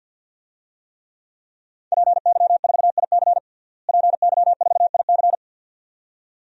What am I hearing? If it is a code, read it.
Morse code, heard as a keyed tone, about 50 wpm, 700 Hz. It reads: PY4IC PY4IC
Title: DXCC-100 Call Signs - 5 Characters - Repeated 2x - 50wpm